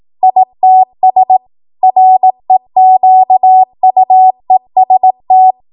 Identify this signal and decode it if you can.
Morse code, heard as a keyed tone, about 18 wpm, 750 Hz. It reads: ITS REQUEST
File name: Morse / 18WPM